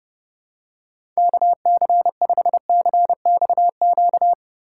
Morse code, heard as a keyed tone, about 30 wpm, 700 Hz. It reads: KC5CXQ
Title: USA Callsign DE Callsign 30wpm